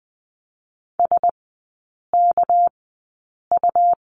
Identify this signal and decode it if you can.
Morse code, heard as a keyed tone, about 20 wpm, 700 Hz. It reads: S K U
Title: 198 Lesson 32e - Letter k - Three Characters 20wpm